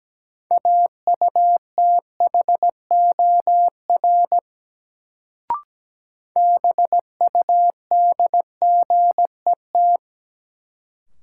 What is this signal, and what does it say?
Morse code, heard as a keyed tone, about 17 wpm, 700 Hz. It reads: AUTHOR BUDGET
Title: Sentences from Top 1200 Words - New Word Drill - 17wpm